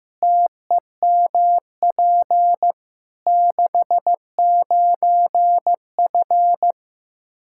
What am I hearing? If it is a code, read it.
Morse code, heard as a keyed tone, about 16 wpm, 700 Hz. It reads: TEMP 69F
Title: K1USN SST QSO Practice 15wpm